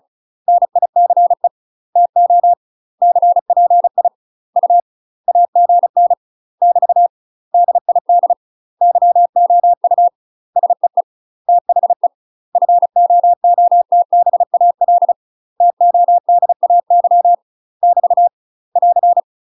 Morse code, heard as a keyed tone, about 36 wpm, 700 Hz. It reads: NICE TO CPI U AGN <BT> DID YOU SEE THE FOOTBAL TODAY <BT> <AR>
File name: UK MASTER QSOs TRAINING / 35WPM / UK MASTER QSOs 0012